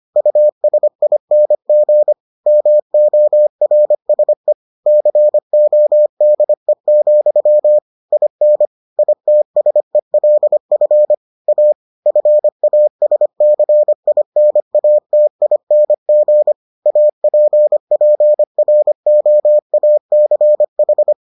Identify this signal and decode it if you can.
Morse code, heard as a keyed tone, about 25 wpm, 600 Hz. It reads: USING MORSE CODE, IN ITSELF A FASCINATING APPROACH